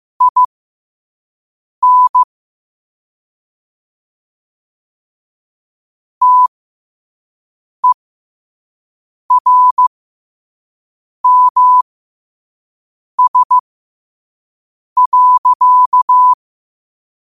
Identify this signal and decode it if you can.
Morse code, heard as a keyed tone, about 15 wpm, spaced as for 5 wpm, 1000 Hz. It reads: IN TERMS.
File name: qotdfiles / 5wpm